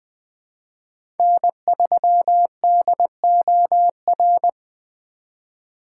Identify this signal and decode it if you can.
Morse code, heard as a keyed tone, about 20 wpm, 700 Hz. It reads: N3DOR
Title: USA Call Signs - 5 Characters - 2x Character Spacing - 20wpm